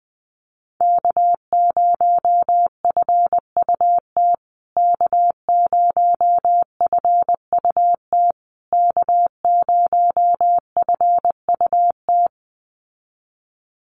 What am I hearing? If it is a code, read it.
Morse code, heard as a keyed tone, about 20 wpm, 700 Hz. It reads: K0FUT K0FUT K0FUT
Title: USA Call Signs Repeated 3x 20wpm